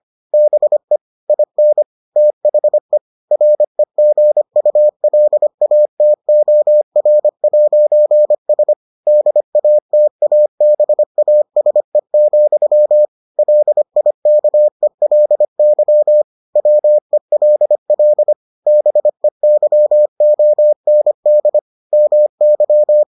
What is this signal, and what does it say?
Morse code, heard as a keyed tone, about 25 wpm, 600 Hz. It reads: BE IN THE REGULATOR'S DATABASE, LIKELY WELL BEYOND MY